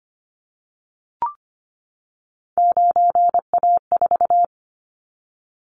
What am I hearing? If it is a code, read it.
Morse code, heard as a keyed tone, about 25 wpm, 700 Hz. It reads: 9A4
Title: Callsign Prefixes for Top 100 DXCC - 25wpm